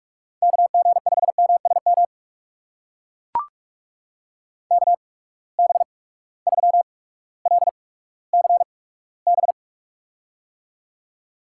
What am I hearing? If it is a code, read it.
Morse code, heard as a keyed tone, about 44 wpm, spaced as for 19 wpm, 700 Hz. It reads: KG5MSM KB3LCB
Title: USA Call Signs - 6 Characters - 8x Character Spacing - 45wpm